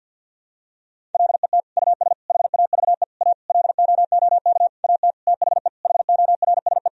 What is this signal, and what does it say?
Morse code, heard as a keyed tone, about 51 wpm, 700 Hz. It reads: LET US HAVE A LOOK AT THE HORSE
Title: Sentences from Top 1000 Words - Review Part 1 - 50wpm